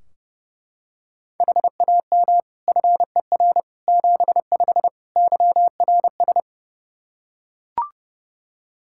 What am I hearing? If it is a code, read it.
Morse code, heard as a keyed tone, about 30 wpm, 700 Hz. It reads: HAM FER 75 YRS
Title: Practice QSO - Exchange 3 - More Info - 30wpm